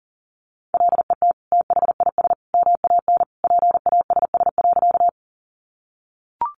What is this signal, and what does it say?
Morse code, heard as a keyed tone, about 40 wpm, 700 Hz. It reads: LET THIS MAN PASS.
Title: Sentences from Top 400 Words 40wpm